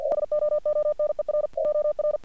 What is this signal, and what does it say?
Morse code, heard as a keyed tone, about 48 wpm, 1200 Hz, with noise. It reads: SOONER OR
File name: archived / CRSnet